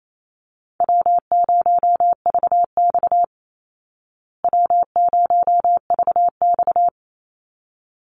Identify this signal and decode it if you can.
Morse code, heard as a keyed tone, about 28 wpm, 700 Hz. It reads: W0VX W0VX
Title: Call Signs Repeated 3x - 28wpm